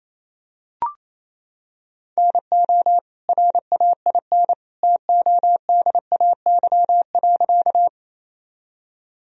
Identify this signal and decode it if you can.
Morse code, heard as a keyed tone, about 27 wpm, 700 Hz. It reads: NO RAIN TODAY.
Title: Sentences from Top 1000 Words - Review Part 1 - 28wpm